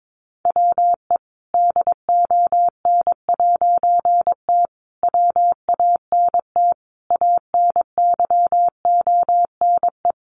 Morse code, heard as a keyed tone, about 22 wpm, 700 Hz. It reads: WE DON'T WANT ANYONE
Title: A Sound Of Thunder 22wpm